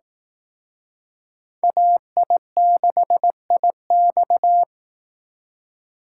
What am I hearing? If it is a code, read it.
Morse code, heard as a keyed tone, about 18 wpm, 700 Hz. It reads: AI6IX
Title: Speed Racing - USA 2x2 Call Signs 15wpm